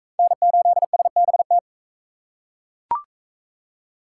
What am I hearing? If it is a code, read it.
Morse code, heard as a keyed tone, about 37 wpm, 700 Hz. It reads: N8SBT